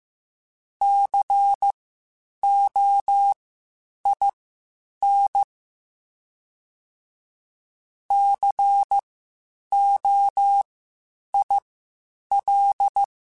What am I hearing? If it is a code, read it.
Morse code, heard as a keyed tone, about 15 wpm, spaced as for 8 wpm, 800 Hz. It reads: COIN COIL
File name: Spaced Alphas / Spaced Alphas mp3s / Cs sp @13